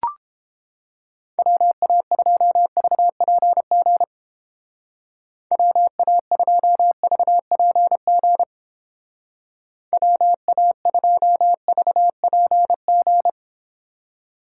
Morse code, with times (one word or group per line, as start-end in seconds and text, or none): 1.4-4.0 WA2VPG
5.5-8.4 WA2VPG
9.9-13.3 WA2VPG